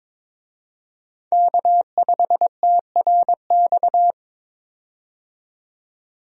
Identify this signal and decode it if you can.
Morse code, heard as a keyed tone, about 22 wpm, 700 Hz. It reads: K5TRX
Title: USA Call Signs - 5 Characters - 22wpm